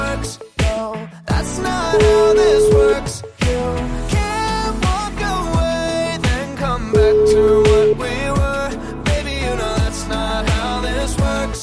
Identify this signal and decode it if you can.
Ringback tone (Europe, 425 Hz), with noise.